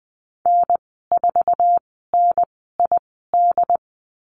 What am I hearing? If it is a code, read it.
Morse code, heard as a keyed tone, about 20 wpm, spaced as for 15 wpm, 700 Hz. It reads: N4NID
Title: 069 Lesson 12g - Letter d - Call Signs 20wpm